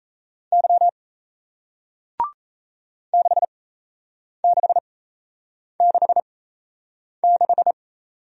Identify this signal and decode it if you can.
Morse code, heard as a keyed tone, about 38 wpm, spaced as for 8 wpm, 700 Hz. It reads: Y 6666